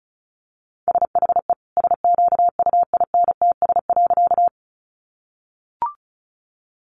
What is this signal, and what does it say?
Morse code, heard as a keyed tone, about 34 wpm, 700 Hz. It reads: SHE SQUINTS.
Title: Additional Easy Phrases 35wpm